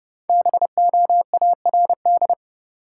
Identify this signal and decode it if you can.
Morse code, heard as a keyed tone, about 30 wpm, 700 Hz. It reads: BOARD
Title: Sentences from Top 700 Words - New Word Drill - 30wpm